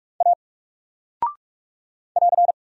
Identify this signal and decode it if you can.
Morse code, heard as a keyed tone, about 45 wpm, 700 Hz. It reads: A <AR>
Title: Single Character (Letter:Number:Punctuation:Prosign) Alt 45wpm